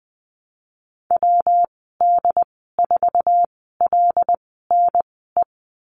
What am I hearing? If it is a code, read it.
Morse code, heard as a keyed tone, about 20 wpm, spaced as for 15 wpm, 700 Hz. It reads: WD4LNE
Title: 167 Lesson 27g - Number 7 - Call Signs 20wpm